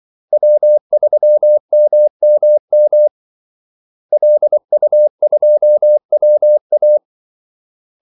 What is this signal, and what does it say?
Morse code, heard as a keyed tone, about 24 wpm, 600 Hz. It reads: W3MMM LU2WA
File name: Callsigns-24wpm